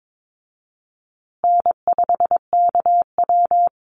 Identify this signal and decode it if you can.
Morse code, heard as a keyed tone, about 22 wpm, 700 Hz. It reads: N5KW
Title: Call Signs Repeated 3x - 22wpm